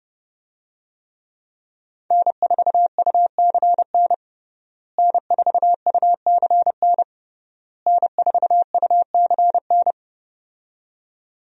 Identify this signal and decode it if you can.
Morse code, heard as a keyed tone, about 31 wpm, 700 Hz. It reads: N4UCN N4UCN N4UCN
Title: USA Call Signs - 5 Characters - Repeated 3x - 30wpm